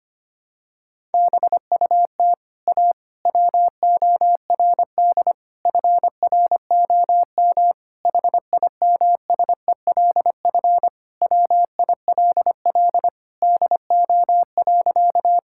Morse code, heard as a keyed tone, about 25 wpm, 700 Hz. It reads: BUT A WORD FROM HIMSELF WILL DO.
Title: Sentences from Top 1600 Words 25wpm